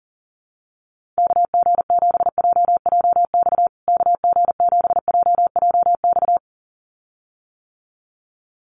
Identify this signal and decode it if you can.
Morse code, heard as a keyed tone, about 40 wpm, 700 Hz. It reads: KG7JJX KG7JJX
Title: USA Call Signs Repeated 2x 40wpm